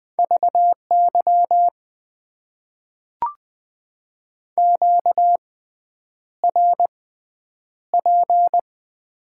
Morse code, heard as a keyed tone, about 20 wpm, spaced as for 9 wpm, 700 Hz. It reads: VY QRP
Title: Top 90 QSO Elements - 6x Character Spacing - 20wpm